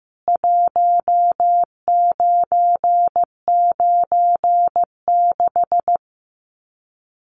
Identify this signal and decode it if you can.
Morse code, heard as a keyed tone, about 15 wpm, 700 Hz. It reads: 1996
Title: Common QSO Phrases 15wpm